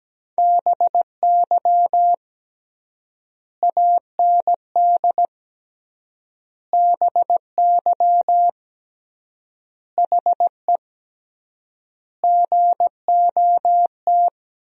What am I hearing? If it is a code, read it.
Morse code, heard as a keyed tone, about 17 wpm, 700 Hz. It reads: BY AND BY HE GOT
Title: 3x Word-Spacing - Sentences from Top 300 Words - 17wpm